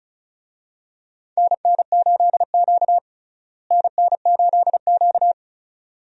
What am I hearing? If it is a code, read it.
Morse code, heard as a keyed tone, about 34 wpm, 700 Hz. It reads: NN8Q NN8Q
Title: USA Call Signs - 4 Characters - Repeated 2x - 35wpm